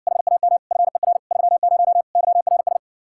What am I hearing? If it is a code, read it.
Morse code, heard as a keyed tone, about 60 wpm, 700 Hz. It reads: HAM FER 30 YRS